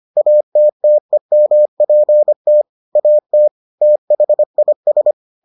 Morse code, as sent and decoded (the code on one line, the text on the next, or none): .- - - . -- .--. - / .- - / - .... .. ...
ATTEMPT AT THIS